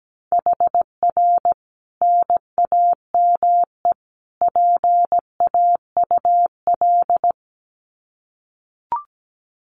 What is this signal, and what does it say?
Morse code, heard as a keyed tone, about 17 wpm, 700 Hz. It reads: HR NAME PAUL
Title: Common QSO Phrases 17wpm